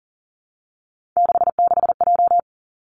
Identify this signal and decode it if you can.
Morse code, heard as a keyed tone, about 40 wpm, 690 Hz. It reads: 66J
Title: Three Challenging Letter-Number Combinations 40wpm